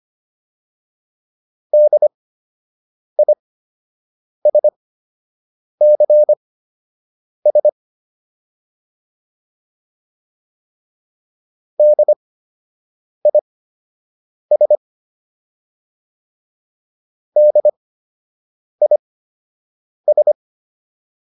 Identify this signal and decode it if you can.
Morse code, heard as a keyed tone, about 25 wpm, spaced as for 7 wpm, 600 Hz. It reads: DISCS DIS DIS